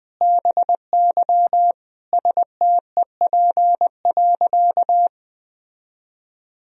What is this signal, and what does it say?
Morse code, heard as a keyed tone, about 20 wpm, 700 Hz. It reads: BY STEP.
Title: Sentences from Top 1000 Words - Review Part 2 - 20wpm